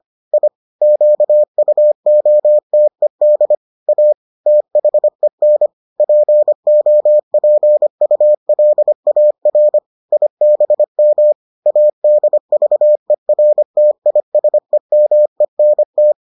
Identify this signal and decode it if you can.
Morse code, heard as a keyed tone, about 25 wpm, 600 Hz. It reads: I QUOTED A THEN POPULAR IBM ADVERTISEMENT